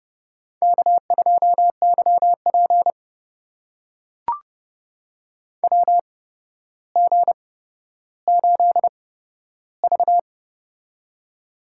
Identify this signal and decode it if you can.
Morse code, heard as a keyed tone, about 30 wpm, spaced as for 13 wpm, 700 Hz. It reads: K2YP WG8V
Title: USA Call Signs - 4 Characters - 8x Character Spacing - 30wpm